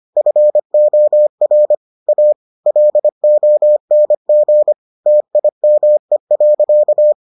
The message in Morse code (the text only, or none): FOR A LONG TIME.